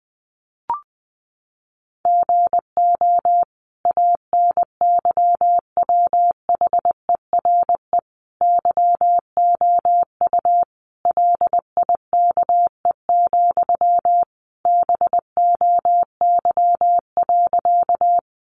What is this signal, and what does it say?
Morse code, heard as a keyed tone, about 20 wpm, 700 Hz. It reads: GO ANYWHERE YOU LIKE, BOY.
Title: Sentences from Top 1400 Words 20wpm